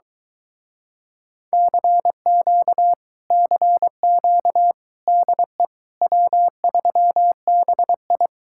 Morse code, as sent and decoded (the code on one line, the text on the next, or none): -.-. --.- / -.-. --.- / -.. / .-- ...-- -... ..
CQ CQ DE W3BI